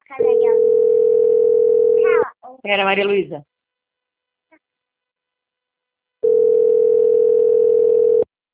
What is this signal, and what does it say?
Ringback tone (North America), 440 + 480 Hz, with speech.
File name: out- - - - - .5799